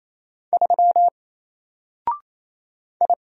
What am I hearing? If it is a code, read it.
Morse code, heard as a keyed tone, about 28 wpm, 700 Hz. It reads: 3 I
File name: Single-Letter-Number-28wpm